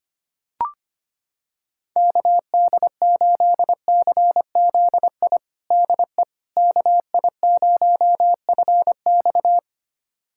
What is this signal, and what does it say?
Morse code, heard as a keyed tone, about 25 wpm, 700 Hz. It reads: KD8CZI DE KI0FX
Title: USA Callsign DE Callsign 25wpm